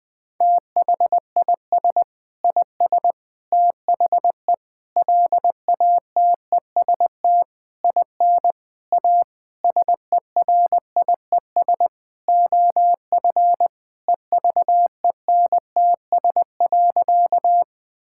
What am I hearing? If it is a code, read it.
Morse code, heard as a keyed tone, about 20 wpm, 700 Hz. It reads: THIS IS THE LATEST IN A SERIES OF EVENTS.
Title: Sentences with 6-Word Phrases - 20wpm